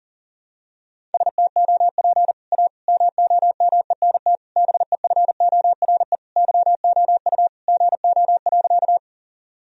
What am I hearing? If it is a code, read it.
Morse code, heard as a keyed tone, about 37 wpm, 700 Hz. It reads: STOP A MOMENT BEFORE YOU GO.
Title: Sentences from Top 1000 Words - Review Part 1 - 40wpm